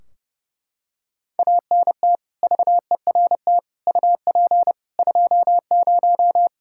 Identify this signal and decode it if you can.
Morse code, heard as a keyed tone, about 30 wpm, 700 Hz. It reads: ANT VERT UP 20
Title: K1USN SST QSO Practice 30wpm